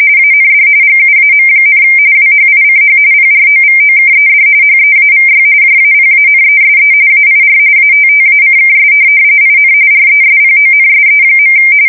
RTTY, about 45.45 baud. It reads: AND GUAM, WHICH PROVOKED HEAVY JAPANESE AIR ATTACKS IN RESPONSE.
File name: fsk170 / USS-Massachusetts